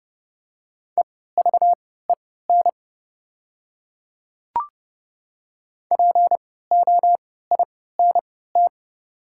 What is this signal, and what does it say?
Morse code, heard as a keyed tone, about 31 wpm, spaced as for 17 wpm, 700 Hz. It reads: EVEN POINT
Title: 3x Character Spacing - Top 300 Words - 30wpm